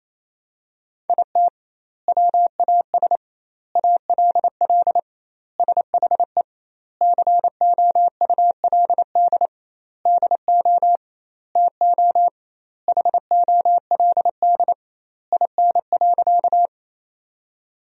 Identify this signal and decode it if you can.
Morse code, heard as a keyed tone, about 28 wpm, 700 Hz. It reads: IT WAS ALL SHE COULD DO TO HOLD IN.
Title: 2x Word-Spacing - Sentences from Top 400 Words - 28wpm